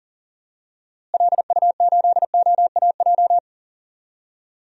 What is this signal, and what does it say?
Morse code, heard as a keyed tone, about 39 wpm, 700 Hz. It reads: LU8OAJ
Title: DXCC-100 Call Signs - 6 Characters - Repeated 2x - 40wpm